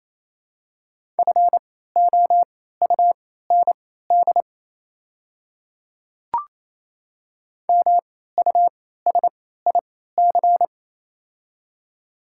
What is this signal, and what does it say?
Morse code, heard as a keyed tone, about 28 wpm, spaced as for 16 wpm, 700 Hz. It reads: FOUND MUSIC
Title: 3x Character Spacing - Top 300 Words - 28wpm